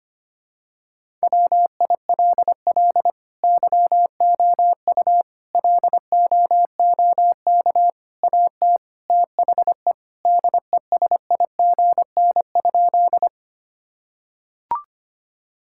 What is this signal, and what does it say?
Morse code, heard as a keyed tone, about 25 wpm, 700 Hz. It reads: WILL YOU LOOK AT THE DESIGN?
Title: Sentences from Top 1000 Words - Review Part 5 - 25wpm